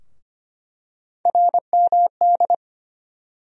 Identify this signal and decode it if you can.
Morse code, heard as a keyed tone, about 25 wpm, 700 Hz. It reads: RMD
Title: USA Call Sign Postfixes - 3 Letters 25wpm